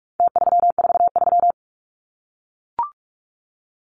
Morse code, heard as a keyed tone, about 44 wpm, 700 Hz. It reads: T343
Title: Numbers - Contest Style - 4-Digit 45wpm